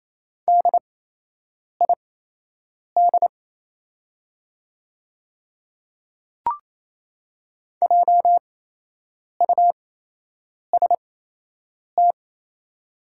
Morse code, heard as a keyed tone, about 28 wpm, spaced as for 8 wpm, 700 Hz. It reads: DID JUST
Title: 8x Character Spacing - Top 200 Words - 28wpm